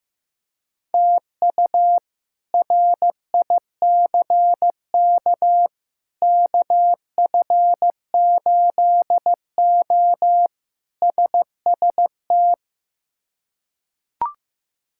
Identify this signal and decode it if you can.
Morse code, heard as a keyed tone, about 15 wpm, 700 Hz. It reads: TU RICK KF8O SST
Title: K1USN SST QSO Practice - 15wpm